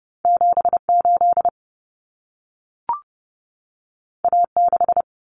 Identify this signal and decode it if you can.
Morse code, heard as a keyed tone, about 30 wpm, 700 Hz. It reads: 78 A6
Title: Common Cut Numbers - 2-Digit 30wpm